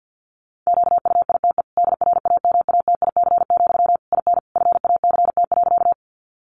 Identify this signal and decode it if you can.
Morse code, heard as a keyed tone, about 50 wpm, 700 Hz. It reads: QUITE DRAMATIC, IN FACT.
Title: Sentences from Top 1300 Words 50wpm